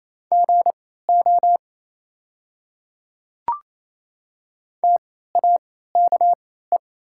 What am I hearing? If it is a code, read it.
Morse code, heard as a keyed tone, about 28 wpm, spaced as for 16 wpm, 700 Hz. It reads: GO TAKE
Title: 3x Character Spacing - Top 300 Words - 28wpm